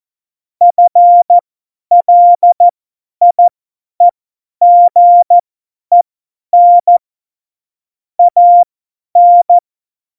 Morse code, heard as a keyed tone, about 14 wpm, spaced as for 10 wpm, 700 Hz. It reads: FLIEGEN AN